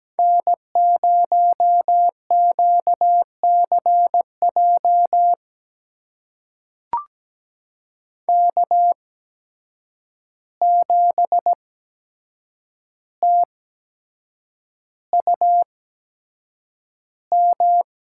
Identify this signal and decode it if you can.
Morse code, heard as a keyed tone, about 17 wpm, spaced as for 7 wpm, 700 Hz. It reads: N0QCJ K7TUM
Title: USA Call Signs - 5 Characters - 8x Character Spacing - 17wpm